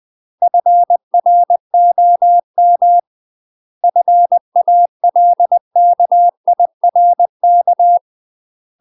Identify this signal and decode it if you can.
Morse code, heard as a keyed tone, about 20 wpm, 700 Hz. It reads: FROM FALKIRK
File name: UK MASTER QSOs TRAINING / 20WPM / UK MASTER QSOs 0010